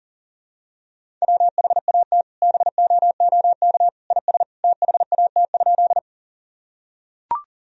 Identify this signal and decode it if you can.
Morse code, heard as a keyed tone, about 40 wpm, 700 Hz. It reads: WHAT BOOK IS THAT?
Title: Sentences from Top 1000 Words - Review Part 1 - 40wpm